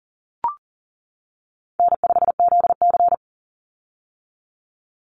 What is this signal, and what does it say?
Morse code, heard as a keyed tone, about 40 wpm, 700 Hz. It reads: N5ZC
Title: Call Signs - 40wpm